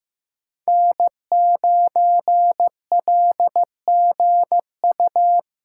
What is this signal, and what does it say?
Morse code, heard as a keyed tone, about 15 wpm, 700 Hz. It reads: N9LGU
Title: USA Callsign DE Callsign - 5 Characters - 15wpm